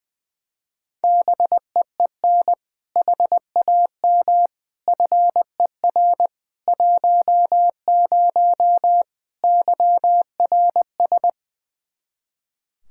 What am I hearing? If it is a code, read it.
Morse code, heard as a keyed tone, about 20 wpm, 700 Hz. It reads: BEEN HAM FER 10 YRS
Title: Practice QSO - Exchange 3 - More Info - 20wpm